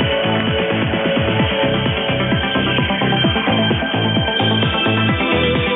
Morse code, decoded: none

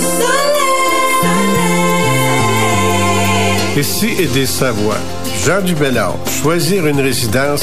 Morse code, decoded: none

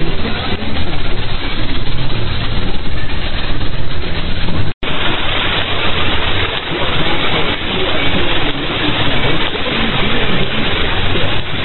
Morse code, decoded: none

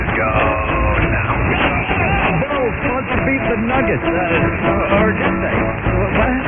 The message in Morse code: none

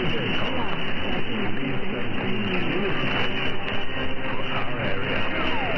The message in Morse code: none

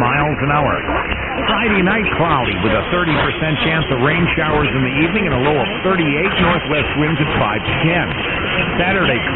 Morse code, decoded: none

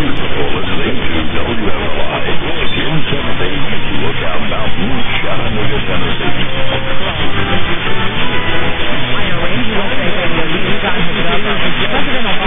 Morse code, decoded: none

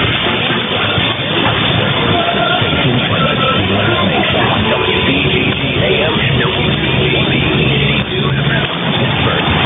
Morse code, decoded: none

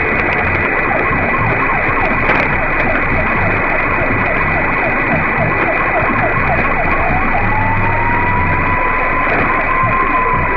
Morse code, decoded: none

